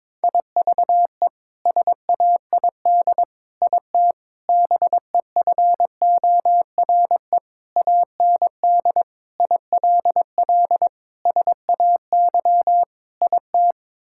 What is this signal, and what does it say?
Morse code, heard as a keyed tone, about 22 wpm, 700 Hz. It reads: IVE SAID IT BEFORE AND ILL SAY IT
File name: practice / 22wpm / Sets of 8 Words - 22wpm